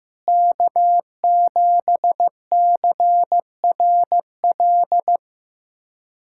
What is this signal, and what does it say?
Morse code, heard as a keyed tone, about 15 wpm, 700 Hz. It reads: K7CRL
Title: USA Callsign DE Callsign - 5 Characters - 15wpm